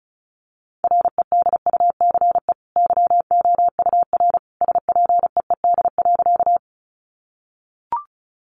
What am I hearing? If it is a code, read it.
Morse code, heard as a keyed tone, about 34 wpm, 700 Hz. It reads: REDUCE YOUR SPEED.
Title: Sentences from Top 1100 Words 35wpm